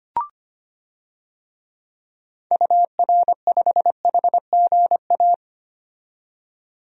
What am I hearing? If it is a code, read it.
Morse code, heard as a keyed tone, about 25 wpm, 690 Hz. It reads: UR5HGA
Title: DXCC-100 Call Signs - 6 Characters - 25wpm